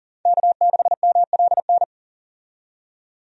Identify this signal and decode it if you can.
Morse code, heard as a keyed tone, about 40 wpm, 700 Hz. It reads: K6MLN
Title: USA Call Signs - 5 Characters - 40wpm